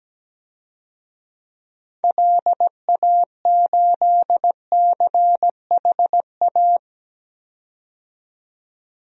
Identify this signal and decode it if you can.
Morse code, heard as a keyed tone, about 17 wpm, 700 Hz. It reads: LA8CHA
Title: DXCC-100 Call Signs - 6 Characters - 17wpm